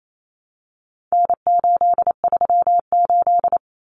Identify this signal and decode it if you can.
Morse code, heard as a keyed tone, about 28 wpm, 700 Hz. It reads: N838
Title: Common Cut Numbers - 4-Digit 28wpm